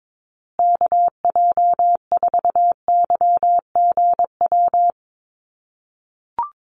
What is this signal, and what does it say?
Morse code, heard as a keyed tone, about 22 wpm, 700 Hz. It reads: KJ4YGW
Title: USA Call Signs Repeated 3x 22wpm